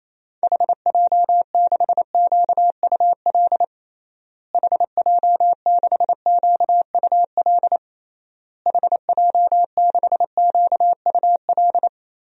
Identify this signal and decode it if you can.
Morse code, heard as a keyed tone, about 28 wpm, 700 Hz. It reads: HJ6QUL HJ6QUL HJ6QUL
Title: DXCC-100 Call Signs - 6 Characters - Repeated 3x - 28wpm